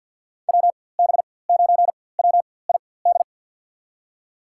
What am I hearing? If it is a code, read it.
Morse code, heard as a keyed tone, about 52 wpm, spaced as for 23 wpm, 700 Hz. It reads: WB9WID
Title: USA Call Signs - 6 Characters - 4x Character Spacing - 50wpm